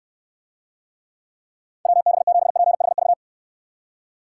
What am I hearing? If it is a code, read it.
Morse code, heard as a keyed tone, about 66 wpm, 700 Hz. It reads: WB7PHV